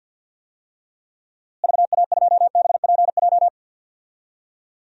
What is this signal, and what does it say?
Morse code, heard as a keyed tone, about 52 wpm, 700 Hz. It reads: VA2BPJ
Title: DXCC-100 Call Signs - 6 Characters - Repeated 3x - 50wpm